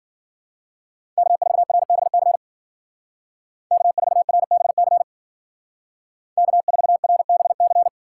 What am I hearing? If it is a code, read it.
Morse code, heard as a keyed tone, about 52 wpm, 700 Hz. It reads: K4RBC K4RBC K4RBC